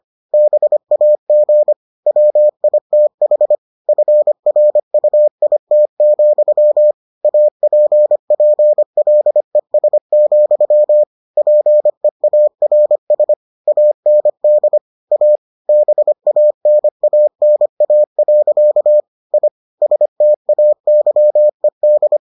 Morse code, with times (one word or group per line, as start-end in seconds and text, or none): 0.3-1.7 BAG
2.1-3.5 WITH
3.9-6.9 FRUIT,
7.2-11.0 APPLES,
11.4-13.3 PEARS
13.7-14.8 AND
15.1-15.4 A
15.7-19.0 BANANA.
19.3-19.5 I
19.8-22.2 STAYED